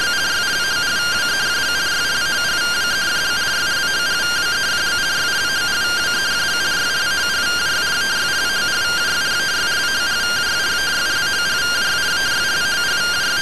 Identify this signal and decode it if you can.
Two-tone teleprinter signal, about 170 Hz shift.